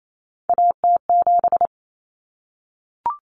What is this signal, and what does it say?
Morse code, heard as a keyed tone, about 28 wpm, 700 Hz. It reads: AT7
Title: Common Cut Numbers - 3-Digit 28wpm